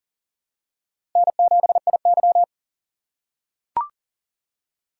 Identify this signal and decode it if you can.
Morse code, heard as a keyed tone, about 40 wpm, 700 Hz. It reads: N7IY
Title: USA Call Signs - 4 Characters - 2x Character Spacing - 40wpm